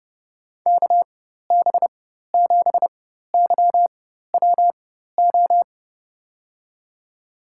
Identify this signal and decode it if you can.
Morse code, heard as a keyed tone, about 30 wpm, spaced as for 14 wpm, 700 Hz. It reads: KB7YWO
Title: USA Call Signs - 6 Characters - 4x Character Spacing - 30wpm